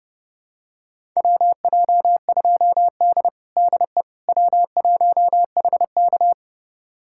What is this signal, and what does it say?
Morse code, heard as a keyed tone, about 32 wpm, 700 Hz. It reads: WJ2D DE W1HK
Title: USA Callsign DE Callsign - 4 Characters - 30wpm